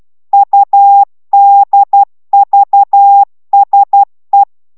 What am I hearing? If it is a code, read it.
Morse code, heard as a keyed tone, about 12 wpm, 800 Hz. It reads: UDVSE